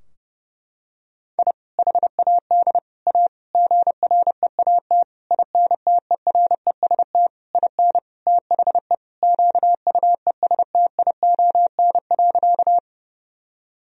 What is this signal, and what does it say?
Morse code, heard as a keyed tone, about 32 wpm, 700 Hz. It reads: I HAD A GREAT INTEREST IN THE QUESTION.
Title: Sentences from Top 1000 Words - Review Part 2 - 30wpm